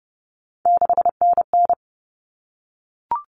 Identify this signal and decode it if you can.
Morse code, heard as a keyed tone, about 30 wpm, 700 Hz. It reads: 6NN
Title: Common Cut Numbers - 3-Digit 30wpm